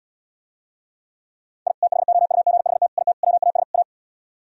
Morse code, heard as a keyed tone, about 77 wpm, 700 Hz. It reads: I THOUGHT IT WAS A